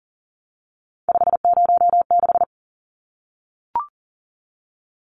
Morse code, heard as a keyed tone, about 40 wpm, 700 Hz. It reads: 506
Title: Numbers - 3-Digit 40wpm